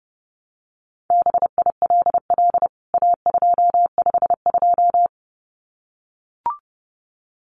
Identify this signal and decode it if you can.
Morse code, heard as a keyed tone, about 30 wpm, 700 Hz. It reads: BILL A252
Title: CWT Exchange 30wpm